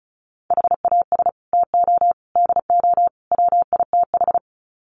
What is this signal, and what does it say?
Morse code, heard as a keyed tone, about 34 wpm, 700 Hz. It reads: HAS TO DO WITH
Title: Sets of 4 Words 35wpm